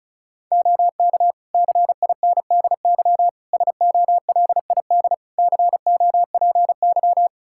Morse code, heard as a keyed tone, about 34 wpm, 700 Hz. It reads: OK CINDY SOLID COPY
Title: K1USN SST QSO Practice 35wpm